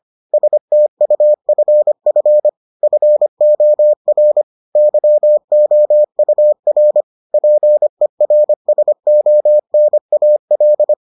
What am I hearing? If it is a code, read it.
Morse code, heard as a keyed tone, about 25 wpm, 600 Hz. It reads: STUFF FOR YOUR PERSONAL